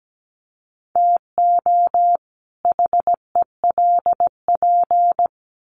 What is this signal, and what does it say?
Morse code, heard as a keyed tone, about 17 wpm, 700 Hz. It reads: TO HELP
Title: Sets of 2 Words 17wpm